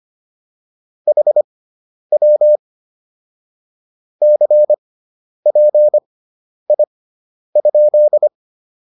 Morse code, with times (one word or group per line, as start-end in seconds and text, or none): 1.1-2.6 HW
4.2-8.3 CPI?